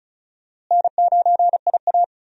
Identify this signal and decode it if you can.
Morse code, heard as a keyed tone, about 36 wpm, 700 Hz. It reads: N9IA
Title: USA Call Signs - 4 Characters - Repeated 2x - 35wpm